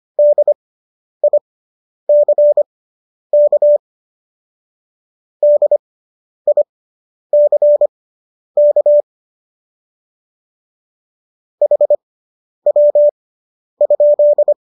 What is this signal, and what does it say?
Morse code, heard as a keyed tone, about 25 wpm, spaced as for 10 wpm, 590 Hz. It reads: DICK DICK HW?